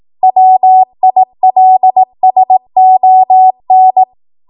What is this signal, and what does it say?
Morse code, heard as a keyed tone, about 18 wpm, 750 Hz. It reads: WILSON
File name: Archive / 18 WPM / 18WPM